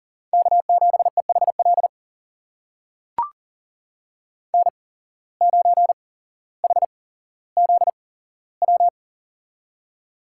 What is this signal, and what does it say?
Morse code, heard as a keyed tone, about 40 wpm, spaced as for 17 wpm, 700 Hz. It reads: K7EHL N9HZW
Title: USA Call Signs - 5 Characters - 8x Character Spacing - 40wpm